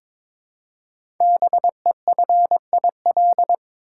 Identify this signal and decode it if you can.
Morse code, heard as a keyed tone, about 22 wpm, 700 Hz. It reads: BEFIL
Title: Code Groups - 5 Letters - 22wpm